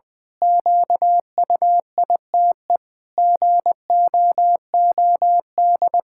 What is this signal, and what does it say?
Morse code, heard as a keyed tone, about 21 wpm, 700 Hz. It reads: QUITE GOOD
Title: Pangram Challenge v2 - 20wpm